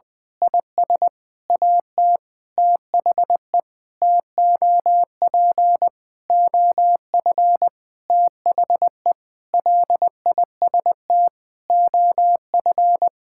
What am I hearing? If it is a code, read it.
Morse code, heard as a keyed tone, about 20 wpm, 700 Hz. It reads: IS AT THE TOP OF THE LIST OF
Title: Sentences with 7-Word Phrases - 20wpm